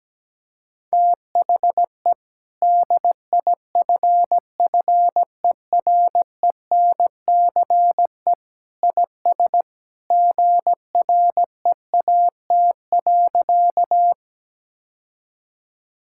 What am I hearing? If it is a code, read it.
Morse code, heard as a keyed tone, about 17 wpm, 700 Hz. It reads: THE DIFFERENCE IS GREAT.
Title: Sentences from Top 1000 Words - Review Part 5 - 17wpm